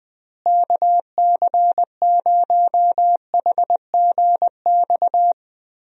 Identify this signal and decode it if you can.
Morse code, heard as a keyed tone, about 20 wpm, 700 Hz. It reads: KC0HGX
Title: USA Call Signs - 6 Characters - Repeated 3x - 20wpm